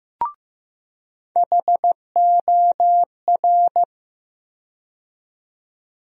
Morse code, heard as a keyed tone, about 15 wpm, 700 Hz. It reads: HOR
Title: USA Call Sign Postfixes - 3 Letters 15wpm